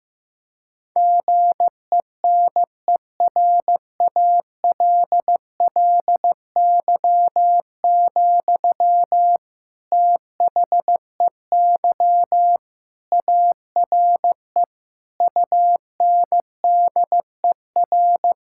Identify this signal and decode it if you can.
Morse code, heard as a keyed tone, about 16 wpm, 700 Hz. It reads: GENERALLY, THEY ARE UNDER
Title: Sentences from Top 1000 Words - Review Part 6 - 15wpm